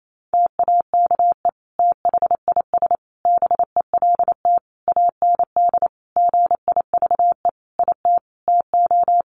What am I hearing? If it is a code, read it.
Morse code, heard as a keyed tone, about 28 wpm, 700 Hz. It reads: TAKE THIS BELT AND GIVE IT TO
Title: Sentences from Top 1400 Words 28wpm